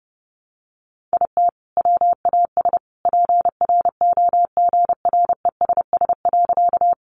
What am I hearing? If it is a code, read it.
Morse code, heard as a keyed tone, about 30 wpm, 700 Hz. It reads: IT WAS PROGRESS.
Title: Sentences from Top 1400 Words 30wpm